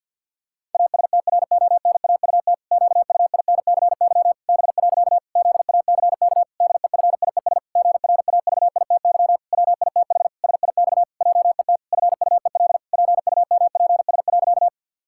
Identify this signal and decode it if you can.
Morse code, heard as a keyed tone, about 49 wpm, 700 Hz. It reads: ASTRONAUT QUINCY B. ZACK DEFIES GRAVITY WITH SIX JET FUEL PUMPS.
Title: Pangram Challenge v2 - 50wpm